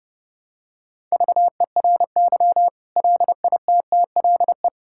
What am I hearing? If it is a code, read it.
Morse code, heard as a keyed tone, about 30 wpm, 700 Hz. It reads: VERY LITTLE
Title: Sentences from Top 1000 Words - Review Part 6 - 30wpm